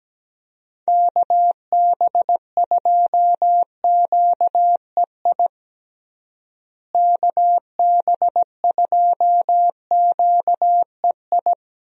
Morse code, heard as a keyed tone, about 17 wpm, 700 Hz. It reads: KB2QEI KB2QEI
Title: USA Call Signs - 6 Characters - Repeated 2x - 17wpm